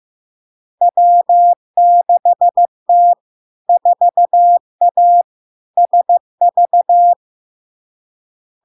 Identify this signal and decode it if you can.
Morse code, heard as a keyed tone, about 15 wpm, 700 Hz. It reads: W6T 4A SV
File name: LARK-FD-2022-Log-in-CW-15wpm-700-Hz